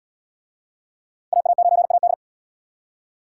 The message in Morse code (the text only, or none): AA9AD